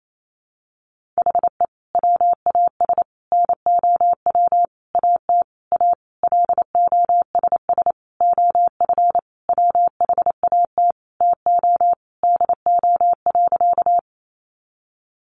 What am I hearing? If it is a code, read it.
Morse code, heard as a keyed tone, about 28 wpm, 700 Hz. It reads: HE WAS NOW AT A LOSS OF WHAT TO DO.
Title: Sentences from Top 1400 Words 28wpm